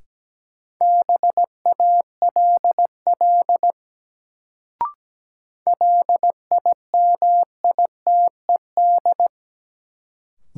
Morse code, heard as a keyed tone, about 17 wpm, 700 Hz. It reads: BALL LIMITED
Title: Sentences from Top 900 Words - New Word Drill - 17wpm